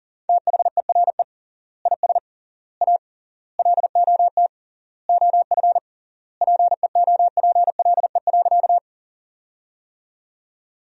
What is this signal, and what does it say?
Morse code, heard as a keyed tone, about 40 wpm, 700 Hz. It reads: THERE IS A LOT OF PEOPLE.
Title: 3x Word-Spacing - Sentences from Top 500 Words - 40wpm